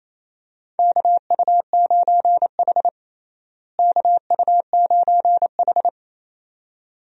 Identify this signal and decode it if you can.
Morse code, heard as a keyed tone, about 28 wpm, 700 Hz. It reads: KU9H KU9H
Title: USA Call Signs - 4 Characters - Repeated 2x - 28wpm